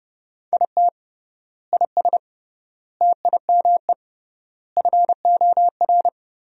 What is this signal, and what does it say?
Morse code, heard as a keyed tone, about 30 wpm, 700 Hz. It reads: IT IS TIME FOR
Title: 3x Word-Spacing - Sentences from Top 500 Words - 30wpm